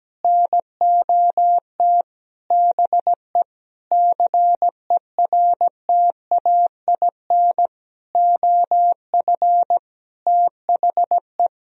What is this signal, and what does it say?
Morse code, heard as a keyed tone, about 17 wpm, 700 Hz. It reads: NOT BE CERTAIN OF THE
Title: Sentences from Top 1000 Words - Review Part 2 - 17wpm